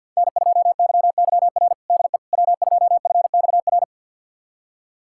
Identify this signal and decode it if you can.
Morse code, heard as a keyed tone, about 50 wpm, 700 Hz. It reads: N2YYR DE W2FXR
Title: USA Callsign DE Callsign - 5 Characters - 50wpm